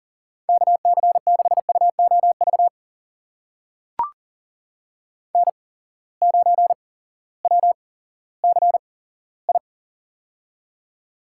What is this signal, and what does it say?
Morse code, heard as a keyed tone, about 37 wpm, spaced as for 18 wpm, 700 Hz. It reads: KC6UOV N9WCI
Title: USA Call Signs - 8x Character Spacing - 40wpm